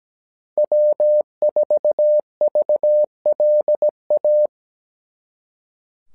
Morse code, heard as a keyed tone, about 17 wpm, 600 Hz. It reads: W4VLA
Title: SOTA QSO Practice 17wpm